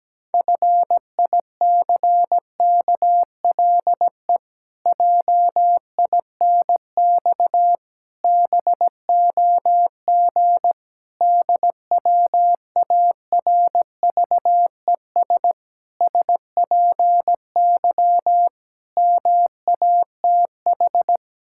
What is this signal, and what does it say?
Morse code, heard as a keyed tone, about 17 wpm, 700 Hz. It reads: FICKLE JINX BOG DWARVES SPY MATH
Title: Pangram Challenge v2 - 17wpm